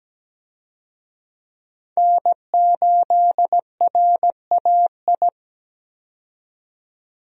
Morse code, heard as a keyed tone, about 17 wpm, 700 Hz. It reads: N8RAI
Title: USA Call Signs - 5 Characters - 17wpm